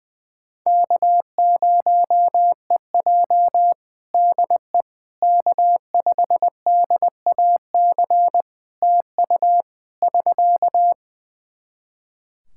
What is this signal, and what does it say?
Morse code, heard as a keyed tone, about 20 wpm, 700 Hz. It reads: K0EJ DE K5DAC TU <SK>
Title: Practice QSO - Exchange 4 - Ending - 20wpm